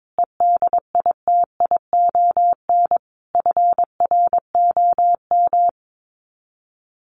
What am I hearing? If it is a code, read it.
Morse code, heard as a keyed tone, about 22 wpm, 700 Hz. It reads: EDITION FROM
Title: Sets of 5 Words Encore2 22wpm